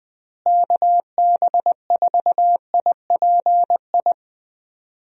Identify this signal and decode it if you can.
Morse code, heard as a keyed tone, about 20 wpm, 700 Hz. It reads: KB4IPI
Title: USA Call Signs - 6 Characters - Repeated 2x - 20wpm